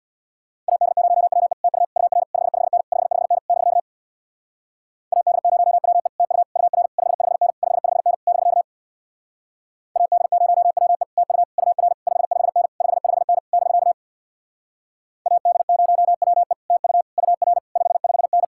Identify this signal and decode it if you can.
Morse code, heard as a keyed tone, about 67 wpm, 700 Hz. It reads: AD0WE TU UR 55N 55N <BK> AD0WE TU UR 55N 55N <BK> AD0WE TU UR 55N 55N <BK> AD0WE TU UR 55N